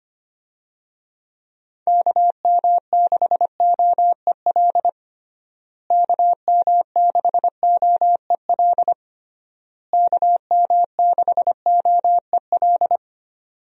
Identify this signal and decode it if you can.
Morse code, heard as a keyed tone, about 25 wpm, 700 Hz. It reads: KM6OEL KM6OEL KM6OEL
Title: USA Call Signs - 6 Characters - Repeated 3x - 25wpm